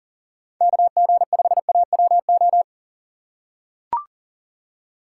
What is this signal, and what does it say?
Morse code, heard as a keyed tone, about 40 wpm, 700 Hz. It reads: KG5AWO
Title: USA Call Signs - 6 Characters - 6x Character Spacing - 40wpm